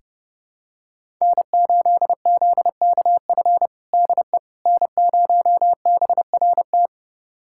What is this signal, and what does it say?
Morse code, heard as a keyed tone, about 32 wpm, 700 Hz. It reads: N8ZKF DE N0BRT
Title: USA Callsign DE Callsign - 5 Characters - 30wpm